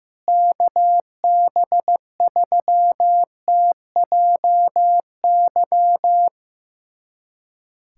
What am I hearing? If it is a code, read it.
Morse code, heard as a keyed tone, about 15 wpm, 700 Hz. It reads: KB3TJY
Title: USA Callsign DE Callsign - 6 Characters - 15wpm